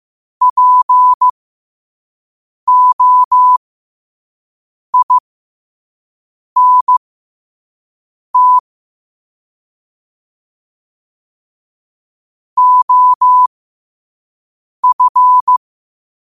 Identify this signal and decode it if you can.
Morse code, heard as a keyed tone, about 15 wpm, spaced as for 5 wpm, 1000 Hz. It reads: POINT OF
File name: qotdfiles / 5wpm